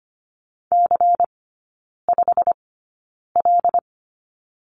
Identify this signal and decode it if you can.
Morse code, heard as a keyed tone, about 25 wpm, spaced as for 9 wpm, 700 Hz. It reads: C5L
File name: course / 25:10wpm / 093 Lesson 16e - Letter c - Three Characters 25wpm